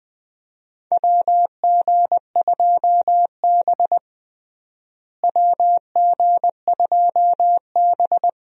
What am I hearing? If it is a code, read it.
Morse code, heard as a keyed tone, about 21 wpm, 700 Hz. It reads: WG2B WG2B
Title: USA Call Signs - 4 Characters - Repeated 2x - 20wpm